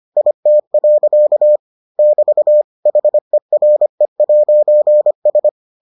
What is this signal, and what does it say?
Morse code, heard as a keyed tone, about 25 wpm, 600 Hz. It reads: IT. <BT> HERE'S